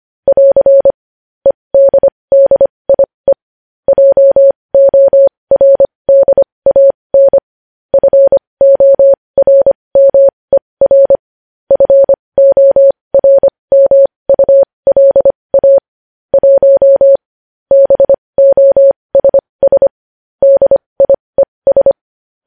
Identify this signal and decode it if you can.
Morse code, heard as a keyed tone, about 25 wpm, spaced as for 20 wpm, 550 Hz. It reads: <AR> EDDIE JORDAN FORMER FORMULA 1 BOSS DIES
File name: podcast / News headlines-1x-25x20